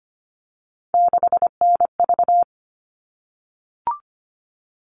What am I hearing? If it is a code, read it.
Morse code, heard as a keyed tone, about 25 wpm, 700 Hz. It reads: 6NV
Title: All Cut Numbers - 3-Digit 25wpm